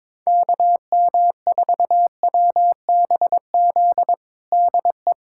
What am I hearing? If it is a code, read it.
Morse code, heard as a keyed tone, about 22 wpm, 700 Hz. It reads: KM4WBZ DE
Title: USA Callsign DE Callsign 22wpm